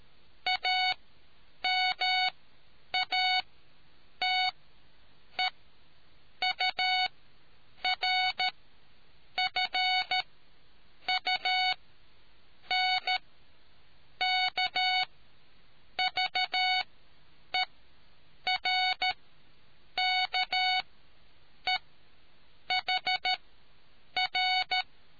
Morse code, heard as a keyed tone, about 13 wpm, spaced as for 7 wpm, 740 Hz. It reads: AMATEURFUNKVERKEHR